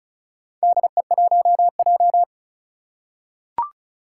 Morse code, heard as a keyed tone, about 36 wpm, 700 Hz. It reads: DE1J
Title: Four Challenging Letter-Number Combinations 35wpm